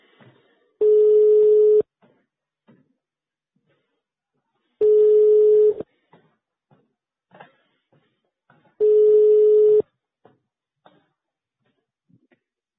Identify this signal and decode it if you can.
Ringback tone (425 Hz, 1 s on, 3 s off).
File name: sauvegarde / ACRRecordings- -jusquau 17-juin-2017 / Police De Huy 2017 05 28 12 38 02